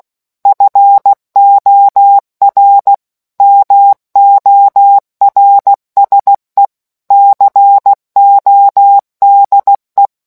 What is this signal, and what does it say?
Morse code, heard as a keyed tone, about 16 wpm, 800 Hz. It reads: FOR MORSE CODE